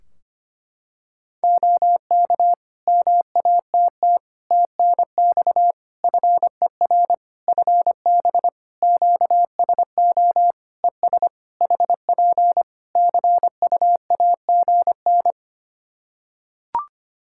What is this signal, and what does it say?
Morse code, heard as a keyed tone, about 25 wpm, 700 Hz. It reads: OK MATT TNX FER FB QSO ES HP CUAGN
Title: Practice QSO - Exchange 4 - Ending - 25wpm